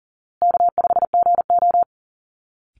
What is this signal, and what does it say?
Morse code, heard as a keyed tone, about 40 wpm, 700 Hz. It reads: K5GO
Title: Call Signs Repeated 2x - 40wpm